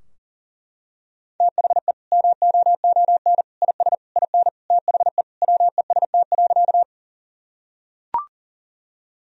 Morse code, heard as a keyed tone, about 40 wpm, 700 Hz. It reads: THE MOON IS IN THE WEST.
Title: Sentences from Top 1000 Words - Review Part 3 - 40wpm